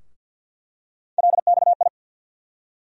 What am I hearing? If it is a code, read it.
Morse code, heard as a keyed tone, about 51 wpm, 700 Hz. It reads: LXI
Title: USA Call Sign Postfixes - 3 Letters 50wpm